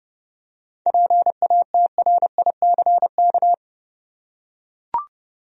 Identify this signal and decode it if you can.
Morse code, heard as a keyed tone, about 30 wpm, 700 Hz. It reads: PATRICK
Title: USA Common Names - 30wpm